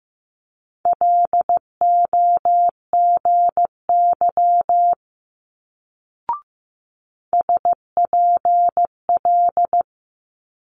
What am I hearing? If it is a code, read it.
Morse code, heard as a keyed tone, about 15 wpm, 700 Hz. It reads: LOGY SPL
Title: Multi-letter Phonemes Quick Fire 15wpm